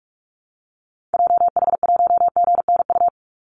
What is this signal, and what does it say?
Morse code, heard as a keyed tone, about 46 wpm, 700 Hz. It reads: JH1GNU
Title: Call Signs - 45wpm